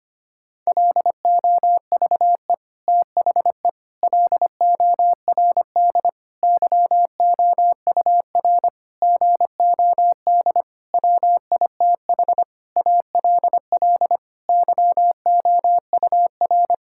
Morse code, heard as a keyed tone, about 25 wpm, 700 Hz. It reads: LOVE THE LORD YOUR GOD WITH ALL YOUR
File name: Sets of 9 Words - 25wpm